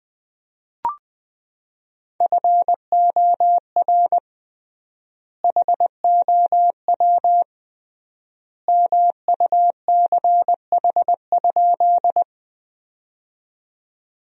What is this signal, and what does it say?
Morse code, heard as a keyed tone, about 20 wpm, 700 Hz. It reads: FOR HOW MUCH?
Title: 3x Word-Spacing - Sentences from Top 200 Words - 20wpm